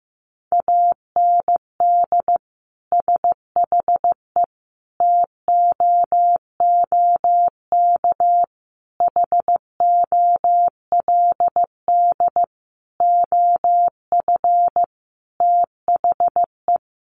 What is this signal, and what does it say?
Morse code, heard as a keyed tone, about 15 wpm, 700 Hz. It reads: AND SHE TOOK HOLD OF THE
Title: Sentences from Top 1200 Words 15wpm